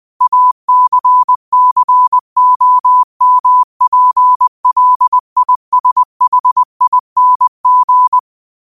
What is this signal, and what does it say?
Morse code, heard as a keyed tone, about 20 wpm, 1000 Hz. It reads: ACCOMPLISHING
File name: qotdfiles / 20wpm